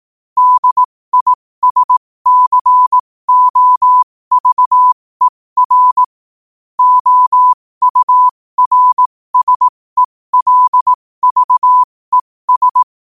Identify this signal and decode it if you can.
Morse code, heard as a keyed tone, about 18 wpm, spaced as for 15 wpm, 1000 Hz. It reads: DISCOVER OURSELVES
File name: qotdfiles / 15wpm